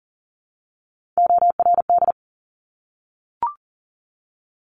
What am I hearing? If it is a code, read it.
Morse code, heard as a keyed tone, about 42 wpm, 700 Hz. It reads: ORD
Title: Top Three Letter Combinations in Words 40wpm